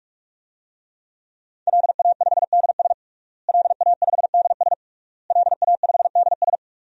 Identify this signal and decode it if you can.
Morse code, heard as a keyed tone, about 46 wpm, 700 Hz. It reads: LA5DS LA5DS LA5DS
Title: DXCC-100 Call Signs - 5 Characters - Repeated 3x - 45wpm